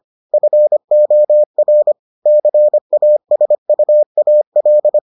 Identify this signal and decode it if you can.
Morse code, heard as a keyed tone, about 25 wpm, 600 Hz. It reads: FOR CASUAL